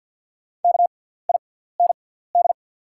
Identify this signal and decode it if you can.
Morse code, heard as a keyed tone, about 48 wpm, spaced as for 17 wpm, 700 Hz. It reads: KIND
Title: 6x Character Spacing - Top 200 Words - 50wpm